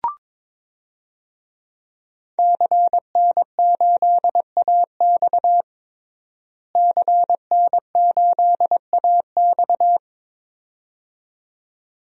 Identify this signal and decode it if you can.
Morse code, heard as a keyed tone, about 22 wpm, 700 Hz. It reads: CN8AX CN8AX
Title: DXCC-100 Call Signs - 5 Characters - Repeated 2x - 22wpm